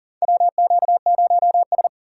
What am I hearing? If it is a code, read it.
Morse code, heard as a keyed tone, about 38 wpm, 700 Hz. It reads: WQ0S
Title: USA Call Signs - 4 Characters - 2x Character Spacing - 40wpm